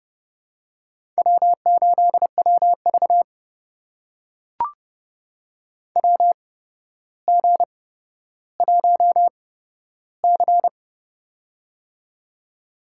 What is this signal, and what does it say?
Morse code, heard as a keyed tone, about 30 wpm, spaced as for 13 wpm, 700 Hz. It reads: W8WV WG1C
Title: USA Call Signs - 4 Characters - 8x Character Spacing - 30wpm